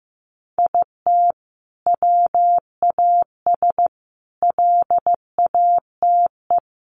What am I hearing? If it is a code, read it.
Morse code, heard as a keyed tone, about 15 wpm, 700 Hz. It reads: IT WAS LATE
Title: Sentences from Top 1300 Words 15wpm